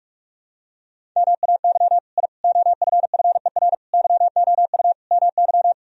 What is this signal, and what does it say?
Morse code, heard as a keyed tone, about 46 wpm, 700 Hz. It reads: MAY I OFFER YOU MY
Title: Sentences from Top 1000 Words - Review Part 4 - 45wpm